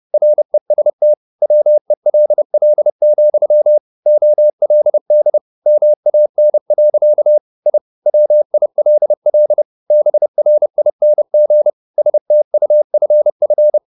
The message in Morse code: REST WELL, OLD MAN. I WILL BRING STUFF

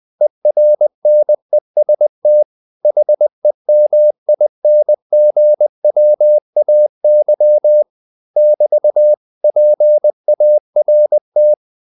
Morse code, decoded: ERNEST HEMINGWAY <BT> PART

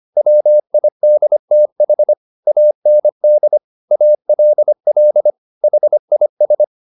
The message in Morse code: WIDTH AND ALL HIS

